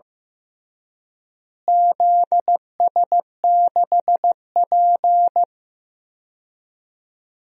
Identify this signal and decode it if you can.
Morse code, heard as a keyed tone, about 16 wpm, 700 Hz. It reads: ZS6P
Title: DXCC-100 Call Signs - 4 Characters - Repeated 3x - 15wpm